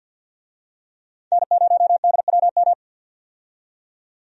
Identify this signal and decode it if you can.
Morse code, heard as a keyed tone, about 48 wpm, 700 Hz. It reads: N0DWM
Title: USA Call Signs - 5 Characters - 50wpm